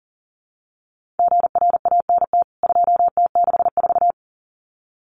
Morse code, heard as a keyed tone, about 40 wpm, 700 Hz. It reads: GRANT 2T64
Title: CWT Exchange 40wpm